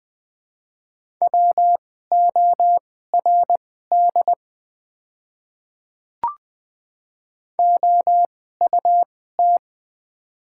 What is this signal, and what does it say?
Morse code, heard as a keyed tone, about 20 wpm, spaced as for 14 wpm, 700 Hz. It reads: WORD OUT
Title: 2x Character Spacing - Top 100 Words - 20wpm